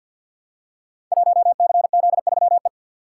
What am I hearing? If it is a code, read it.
Morse code, heard as a keyed tone, about 53 wpm, 700 Hz. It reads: 1CZ3E
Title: Code Groups - 5 Letter:Numbers - 50wpm